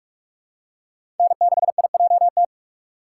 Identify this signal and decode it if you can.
Morse code, heard as a keyed tone, about 46 wpm, 700 Hz. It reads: N6IJT
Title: USA Call Signs - 5 Characters - 2x Character Spacing - 45wpm